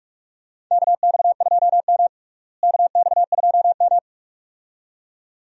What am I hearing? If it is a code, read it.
Morse code, heard as a keyed tone, about 43 wpm, 700 Hz. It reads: KX2M KX2M
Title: USA Call Signs - 4 Characters - Repeated 2x - 45wpm